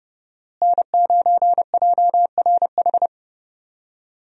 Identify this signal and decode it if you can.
Morse code, heard as a keyed tone, about 30 wpm, 700 Hz. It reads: N9JRH
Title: USA Call Signs - 5 Characters - Repeated 2x - 30wpm